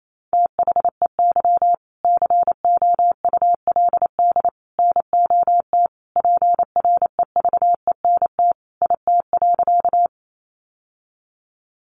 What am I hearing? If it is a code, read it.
Morse code, heard as a keyed tone, about 28 wpm, 700 Hz. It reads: THEY COULD NOT PREVENT IT.
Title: Sentences from Top 1100 Words 28wpm